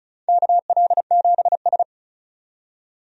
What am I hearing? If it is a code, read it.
Morse code, heard as a keyed tone, about 36 wpm, 700 Hz. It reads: KL7S
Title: DXCC-100 Call Signs - 4 Characters - 35wpm